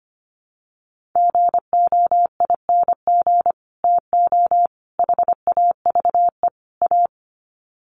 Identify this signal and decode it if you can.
Morse code, heard as a keyed tone, about 25 wpm, 700 Hz. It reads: GOING TO HAVE A
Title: Sets of 4 Words 25wpm